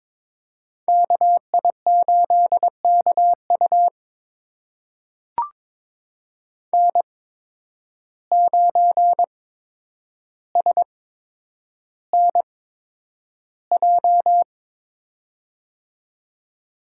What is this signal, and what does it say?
Morse code, heard as a keyed tone, about 22 wpm, spaced as for 9 wpm, 700 Hz. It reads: KI8KU N9SNJ
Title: USA Call Signs - 8x Character Spacing - 22wpm